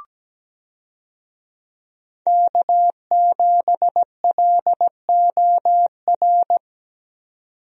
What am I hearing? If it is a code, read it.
Morse code, heard as a keyed tone, about 17 wpm, 700 Hz. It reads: K7LOR
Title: USA Call Signs - 5 Characters - 17wpm